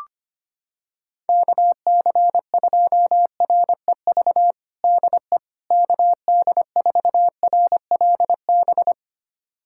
Morse code, heard as a keyed tone, about 25 wpm, 700 Hz. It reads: KC2REV DE KD4RLB
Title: USA Callsign DE Callsign - 6 Characters - 25wpm